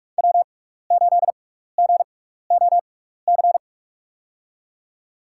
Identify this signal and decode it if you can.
Morse code, heard as a keyed tone, about 44 wpm, spaced as for 15 wpm, 690 Hz. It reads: W8GOC
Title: USA Call Signs - 6x Character Spacing - 45wpm